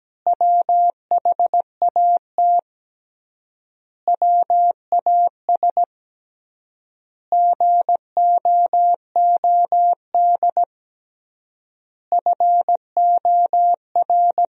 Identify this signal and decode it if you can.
Morse code, heard as a keyed tone, about 17 wpm, 700 Hz. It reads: WHAT WAS GOOD FOR
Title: 3x Word-Spacing - Sentences from Top 400 Words - 17wpm